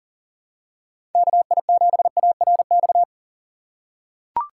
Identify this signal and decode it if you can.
Morse code, heard as a keyed tone, about 40 wpm, 700 Hz. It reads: KI7ARX
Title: USA Call Signs - 6 Characters - 6x Character Spacing - 40wpm